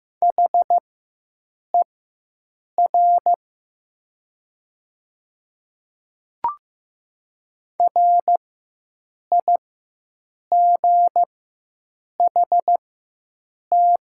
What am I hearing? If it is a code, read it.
Morse code, heard as a keyed tone, about 15 wpm, spaced as for 7 wpm, 700 Hz. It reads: HER RIGHT
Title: 4x Character Spacing - Top 500 Words - 15wpm